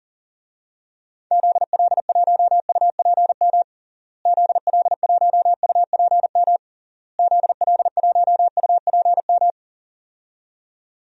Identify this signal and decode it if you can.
Morse code, heard as a keyed tone, about 40 wpm, 700 Hz. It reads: ZL1UPM ZL1UPM ZL1UPM
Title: DXCC-100 Call Signs - 6 Characters - Repeated 3x - 40wpm